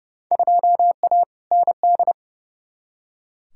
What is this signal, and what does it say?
Morse code, heard as a keyed tone, about 30 wpm, 700 Hz. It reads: 2A ND
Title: Field Day Exchanges Encore - 30wpm